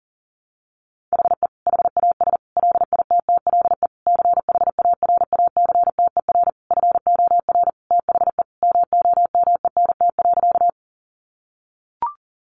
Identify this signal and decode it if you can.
Morse code, heard as a keyed tone, about 40 wpm, 700 Hz. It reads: HE HAS LITTLE CHARACTER FOR THE MOMENT.
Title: Sentences from Top 1200 Words 40wpm